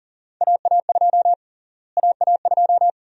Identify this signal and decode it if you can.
Morse code, heard as a keyed tone, about 38 wpm, 700 Hz. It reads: AA2 AA2
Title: USA Call Sign Prefixes - Repeated 2x - 40wpm